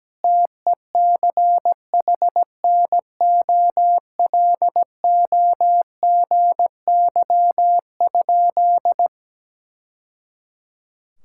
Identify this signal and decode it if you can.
Morse code, heard as a keyed tone, about 17 wpm, 700 Hz. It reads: TECHNOLOGY?
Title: Sentences from Top 1000 Words - Review Part 5 - 17wpm